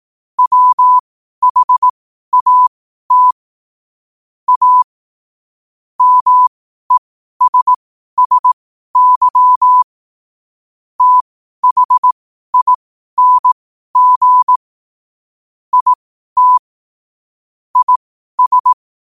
Morse code, heard as a keyed tone, about 18 wpm, spaced as for 12 wpm, 1000 Hz. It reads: WHAT A MESSY THING IT IS
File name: qotdfiles / 12wpm